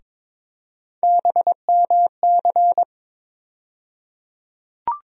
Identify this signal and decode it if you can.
Morse code, heard as a keyed tone, about 22 wpm, 700 Hz. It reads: BMC
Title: USA Call Sign Postfixes - 3 Letters 22wpm